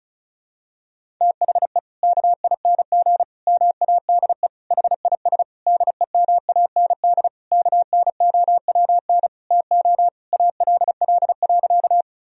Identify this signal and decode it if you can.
Morse code, heard as a keyed tone, about 34 wpm, 700 Hz. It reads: THE KING MADE HIS DEMAND KNOWN TO ALL.
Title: Sentences from Top 1000 Words - Review Part 4 - 35wpm